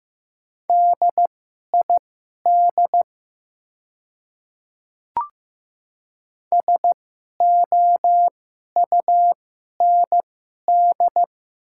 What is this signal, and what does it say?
Morse code, heard as a keyed tone, about 15 wpm, spaced as for 11 wpm, 700 Hz. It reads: DID SOUND
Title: 2x Character Spacing - Top 200 Words - 15wpm